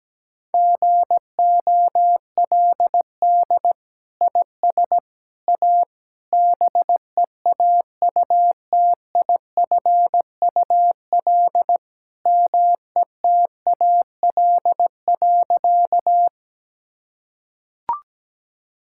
Morse code, heard as a keyed tone, about 17 wpm, 700 Hz. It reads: GOLD IS A BEAUTIFUL METAL.
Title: Sentences from Top 1000 Words - Review Part 6 - 17wpm